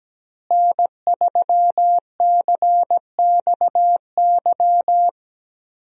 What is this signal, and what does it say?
Morse code, heard as a keyed tone, about 17 wpm, 700 Hz. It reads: N3CXY
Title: USA Call Signs - 5 Characters - Repeated 3x - 17wpm